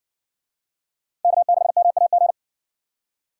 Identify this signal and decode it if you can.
Morse code, heard as a keyed tone, about 52 wpm, 700 Hz. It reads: K6MAG